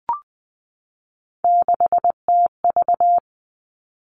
Morse code, heard as a keyed tone, about 20 wpm, 700 Hz. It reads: 6TV
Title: All Cut Numbers - 3-Digit 20wpm